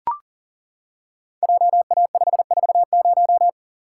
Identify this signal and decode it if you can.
Morse code, heard as a keyed tone, about 40 wpm, 700 Hz. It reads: JA540
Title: Code Groups - 5 Letter:Numbers - 40wpm